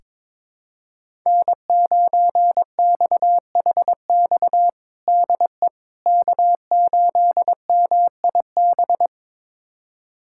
Morse code, heard as a keyed tone, about 22 wpm, 700 Hz. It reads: N9XHX DE K8MIB
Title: USA Callsign DE Callsign - 5 Characters - 22wpm